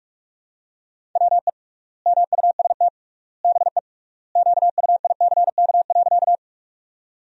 Morse code, heard as a keyed tone, about 41 wpm, 700 Hz. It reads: WE MUST BE QUICK.
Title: 3x Word-Spacing - Sentences from Top 500 Words - 45wpm